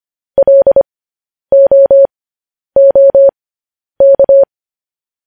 Morse code, heard as a keyed tone, about 25 wpm, spaced as for 10 wpm, 550 Hz. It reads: LOOK
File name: podcast / News headlines-1x-25x10